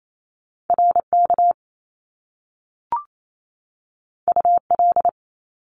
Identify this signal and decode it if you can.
Morse code, heard as a keyed tone, about 28 wpm, 700 Hz. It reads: RK UL
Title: Top Two Letter Combinations in Words 28wpm